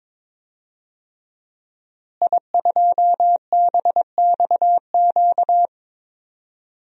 Morse code, heard as a keyed tone, about 22 wpm, 700 Hz. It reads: I2BXQ
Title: DXCC-100 Call Signs - 5 Characters - Repeated 3x - 22wpm